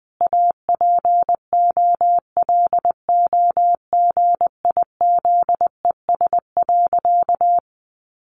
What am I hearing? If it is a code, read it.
Morse code, heard as a keyed tone, about 20 wpm, 700 Hz. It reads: APOLOGIZES.
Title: Additional Easy Phrases 20wpm